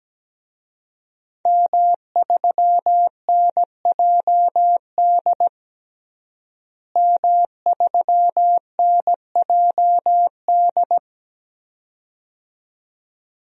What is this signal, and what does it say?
Morse code, heard as a keyed tone, about 17 wpm, 700 Hz. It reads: M3NJD M3NJD
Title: DXCC-100 Call Signs - 5 Characters - Repeated 2x - 17wpm